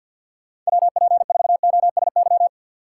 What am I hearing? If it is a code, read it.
Morse code, heard as a keyed tone, about 65 wpm, spaced as for 57 wpm, 700 Hz. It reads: WP4OSY
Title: Speed Racing - USA 2x3 Call Signs 50wpm